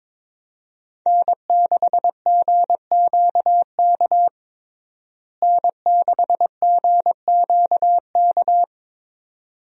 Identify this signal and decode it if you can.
Morse code, heard as a keyed tone, about 22 wpm, 700 Hz. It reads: N6GQK N6GQK
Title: USA Call Signs - 5 Characters - Repeated 2x - 22wpm